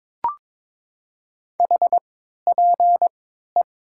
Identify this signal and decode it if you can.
Morse code, heard as a keyed tone, about 22 wpm, 700 Hz. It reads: H P E
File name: Top 90 QSO Elements - 3x Character Spacing - 22wpm